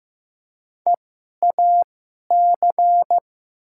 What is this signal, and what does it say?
Morse code, heard as a keyed tone, about 15 wpm, spaced as for 11 wpm, 700 Hz. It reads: EAC
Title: 2x Character Spacing - Top 100 Words - 15wpm